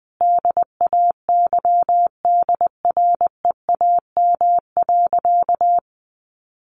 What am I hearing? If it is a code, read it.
Morse code, heard as a keyed tone, about 20 wpm, 700 Hz. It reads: DAYDREAM.
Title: Easy Phrases 20wpm